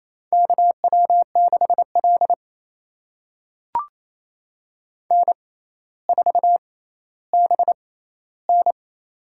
Morse code, heard as a keyed tone, about 28 wpm, spaced as for 14 wpm, 700 Hz. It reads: KW6L N4BN
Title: USA Call Signs - 4 Characters - 6x Character Spacing - 28wpm